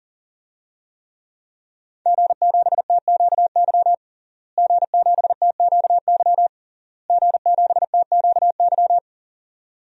Morse code, heard as a keyed tone, about 37 wpm, 700 Hz. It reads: G7TQY G7TQY G7TQY
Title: DXCC-100 Call Signs - 5 Characters - Repeated 3x - 40wpm